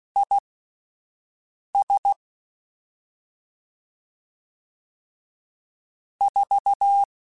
Morse code, heard as a keyed tone, about 16 wpm, spaced as for 5 wpm, 800 Hz. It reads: IS 4
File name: QSOs Spaced